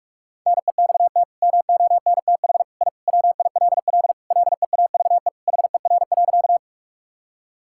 Morse code, heard as a keyed tone, about 46 wpm, 700 Hz. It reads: NEXT MONTH I WILL LEAVE HER.
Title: Sentences from Top 1000 Words - Review Part 5 - 45wpm